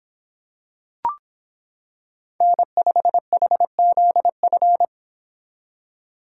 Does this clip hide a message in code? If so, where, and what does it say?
Yes: Morse code from 2.4 to 4.8 s, heard as a keyed tone, about 26 wpm, 700 Hz. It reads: N5HZF